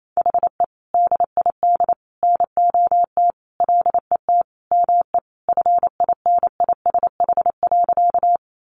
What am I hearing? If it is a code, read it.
Morse code, heard as a keyed tone, about 28 wpm, 700 Hz. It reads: HE DID NOT LET ME FINISH.
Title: Sentences from Top 1200 Words 28wpm